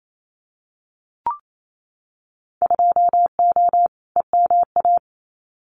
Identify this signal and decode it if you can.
Morse code, heard as a keyed tone, about 28 wpm, 700 Hz. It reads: 2O EMA
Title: Winter Field Day Exchanges 28wpm